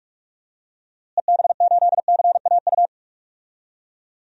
Morse code, heard as a keyed tone, about 46 wpm, 700 Hz. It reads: EB8CAU
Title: DXCC-100 Call Signs - 6 Characters - Repeated 3x - 45wpm